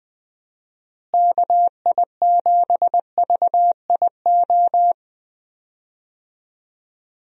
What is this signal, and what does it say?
Morse code, heard as a keyed tone, about 20 wpm, 700 Hz. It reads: KI7VIO
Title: USA Call Signs - 6 Characters - Repeated 3x - 20wpm